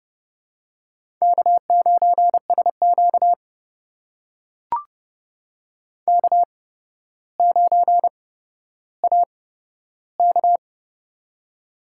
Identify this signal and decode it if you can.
Morse code, heard as a keyed tone, about 30 wpm, spaced as for 13 wpm, 700 Hz. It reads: K9SQ K9AK
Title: USA Call Signs - 4 Characters - 8x Character Spacing - 30wpm